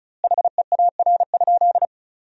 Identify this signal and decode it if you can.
Morse code, heard as a keyed tone, about 34 wpm, 700 Hz. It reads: HEAR?
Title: 2x Word-Spacing - Sentences from Top 300 Words - 35wpm